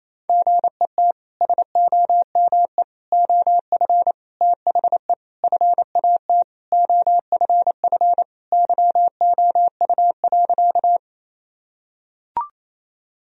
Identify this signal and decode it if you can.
Morse code, heard as a keyed tone, about 27 wpm, 700 Hz. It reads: GET SOME OF THE FAT OFF YOU.
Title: Sentences from Top 1000 Words - Review Part 4 - 28wpm